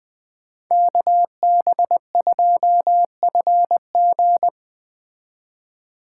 Morse code, heard as a keyed tone, about 20 wpm, 700 Hz. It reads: KB2FG
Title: Speed Racing - USA 2x2 Call Signs 17wpm